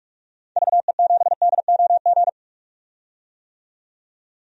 Morse code, heard as a keyed tone, about 46 wpm, 700 Hz. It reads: VE7DOG
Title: DXCC-100 Call Signs - 6 Characters - 45wpm